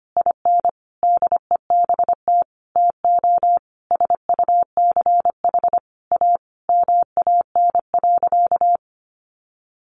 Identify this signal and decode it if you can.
Morse code, heard as a keyed tone, about 25 wpm, 700 Hz. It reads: IN DEBT TO SUCH A MAN.
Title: Sentences from Top 1300 Words 25wpm